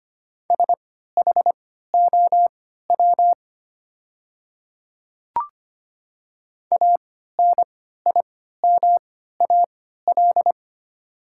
Morse code, heard as a keyed tone, about 25 wpm, spaced as for 14 wpm, 690 Hz. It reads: SHOW ANIMAL
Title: 3x Character Spacing - Top 200 Words - 25wpm